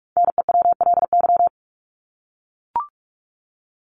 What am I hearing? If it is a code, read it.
Morse code, heard as a keyed tone, about 44 wpm, 700 Hz. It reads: NEWLY
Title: LY Suffix - Top 100 Words with Base Words 45wpm